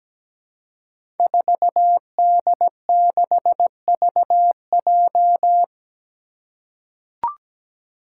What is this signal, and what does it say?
Morse code, heard as a keyed tone, about 17 wpm, 700 Hz. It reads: 4D6VJ
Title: Five Challenging Letter-Number Combinations 17wpm